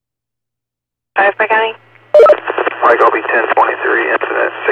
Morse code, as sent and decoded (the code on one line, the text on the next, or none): none